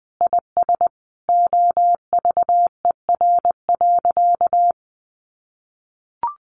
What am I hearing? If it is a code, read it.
Morse code, heard as a keyed tone, about 20 wpm, 700 Hz. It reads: IS OVER.
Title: Sentences from Top 1500 Words 20wpm